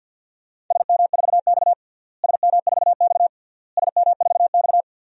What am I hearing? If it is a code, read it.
Morse code, heard as a keyed tone, about 51 wpm, 700 Hz. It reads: SM4X SM4X SM4X
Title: DXCC-100 Call Signs - 4 Characters - Repeated 3x - 50wpm